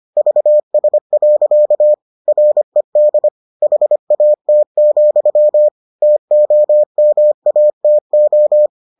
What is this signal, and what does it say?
Morse code, heard as a keyed tone, about 25 wpm, 600 Hz. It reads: VS. RED HAT, TOMATO